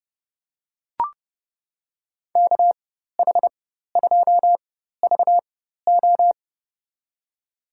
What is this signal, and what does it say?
Morse code, heard as a keyed tone, about 30 wpm, spaced as for 14 wpm, 700 Hz. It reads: KH2VO
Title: USA Call Signs - 5 Characters - 4x Character Spacing - 30wpm